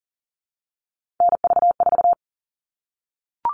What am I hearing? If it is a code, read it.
Morse code, heard as a keyed tone, about 40 wpm, 700 Hz. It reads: NV4
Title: All Cut Numbers - 3-Digit 40wpm